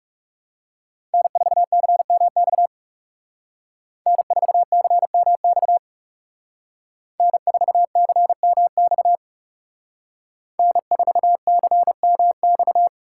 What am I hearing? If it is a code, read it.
Morse code, heard as a keyed tone, about 35 wpm, 700 Hz. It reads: N4CMX N4CMX N4CMX N4CMX
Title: Speed Racing - USA 1x3 Call Signs 30wpm